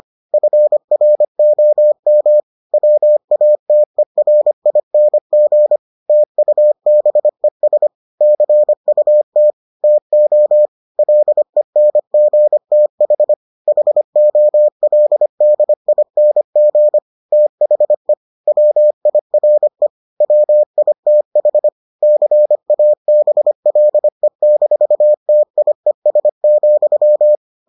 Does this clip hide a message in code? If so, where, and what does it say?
Yes: Morse code from 0.3 to 27.4 s, heard as a keyed tone, about 25 wpm, 600 Hz. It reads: FROM WATERING TUBES CUT TO LENGTH HOLDING THE WIRE WITH CABLE-TIES,